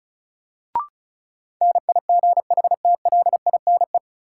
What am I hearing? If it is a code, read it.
Morse code, heard as a keyed tone, about 34 wpm, 700 Hz. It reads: NIGHTLINE
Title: Sets of 5 Words - New Word Drill - 35wpm